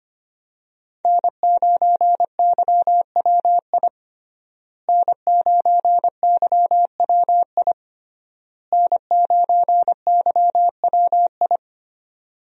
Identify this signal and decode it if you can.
Morse code, heard as a keyed tone, about 25 wpm, 700 Hz. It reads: N9YWI N9YWI N9YWI
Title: USA Call Signs - 5 Characters - Repeated 3x - 25wpm